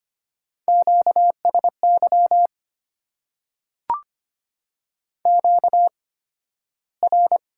Morse code, heard as a keyed tone, about 25 wpm, spaced as for 13 wpm, 700 Hz. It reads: QSY QR
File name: Top 90 QSO Elements - 8x Character Spacing - 25wpm